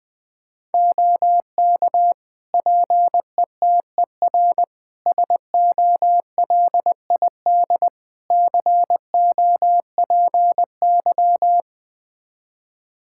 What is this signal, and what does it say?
Morse code, heard as a keyed tone, about 21 wpm, 700 Hz. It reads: OK PETER SOLID COPY
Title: K1USN SST QSO Practice 20wpm